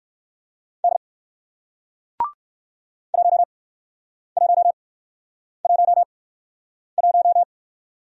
Morse code, heard as a keyed tone, about 56 wpm, spaced as for 9 wpm, 700 Hz. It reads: D 1111
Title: Speed-Racing Single Letter-Number 45wpm